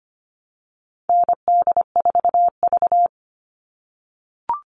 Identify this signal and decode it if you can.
Morse code, heard as a keyed tone, about 25 wpm, 700 Hz. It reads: ND4V
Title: All Cut Numbers - 4-Digit 25wpm